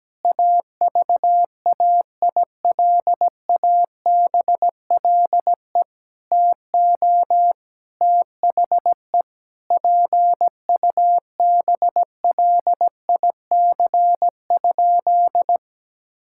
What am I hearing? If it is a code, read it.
Morse code, heard as a keyed tone, about 17 wpm, 700 Hz. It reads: AVAILABLE TO THE PUBLIC?
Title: Sentences from Top 1000 Words - Review Part 5 - 17wpm